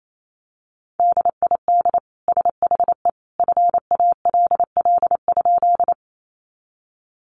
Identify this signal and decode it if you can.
Morse code, heard as a keyed tone, about 28 wpm, 700 Hz. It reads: DID SHE FALL?
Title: Sentences from Top 500 Words 28wpm